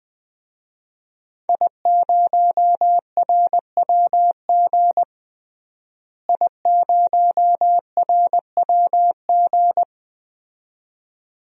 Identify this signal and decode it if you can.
Morse code, heard as a keyed tone, about 20 wpm, 700 Hz. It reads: I0RWG I0RWG
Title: DXCC-100 Call Signs - 5 Characters - Repeated 2x - 20wpm